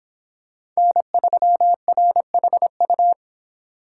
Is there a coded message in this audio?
Yes: Morse code from 0.8 to 3.1 s, heard as a keyed tone, about 26 wpm, 700 Hz. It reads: N3RHU